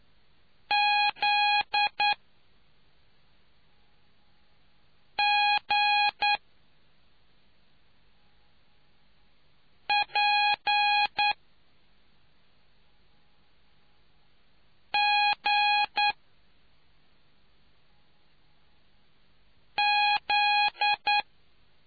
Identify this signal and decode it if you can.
Morse code, heard as a keyed tone, about 9 wpm, spaced as for 2 wpm, 830 Hz. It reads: ZGPGZ